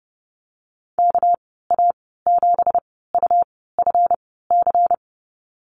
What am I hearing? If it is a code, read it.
Morse code, heard as a keyed tone, about 30 wpm, spaced as for 17 wpm, 700 Hz. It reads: KA7UFC
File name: Lesson 37g - Letter x - Call Signs 30wpm